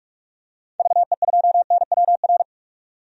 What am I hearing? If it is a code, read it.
Morse code, heard as a keyed tone, about 45 wpm, 700 Hz. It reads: VE2NWR
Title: DXCC-100 Call Signs - 6 Characters - 45wpm